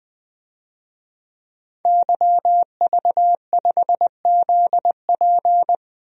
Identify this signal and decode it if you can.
Morse code, heard as a keyed tone, about 20 wpm, 700 Hz. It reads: YV5ZP
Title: DXCC-100 Call Signs - 5 Characters - 20wpm